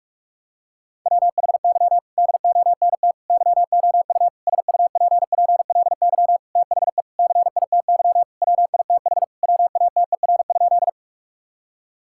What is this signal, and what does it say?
Morse code, heard as a keyed tone, about 41 wpm, 700 Hz. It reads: WHY DONT YOU SUPPLY THE CITY WITH WATER?
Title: Sentences from Top 1000 Words - Review Part 5 - 45wpm